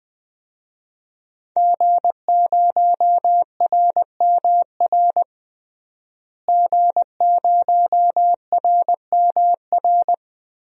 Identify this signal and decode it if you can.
Morse code, heard as a keyed tone, about 20 wpm, 700 Hz. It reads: G0RMR G0RMR
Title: DXCC-100 Call Signs - 5 Characters - Repeated 3x - 20wpm